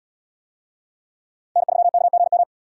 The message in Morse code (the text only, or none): N3KGM